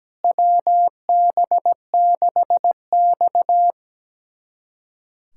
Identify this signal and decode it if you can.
Morse code, heard as a keyed tone, about 17 wpm, 700 Hz. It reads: WB6X
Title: USA Callsign DE Callsign - 4 Characters - 17wpm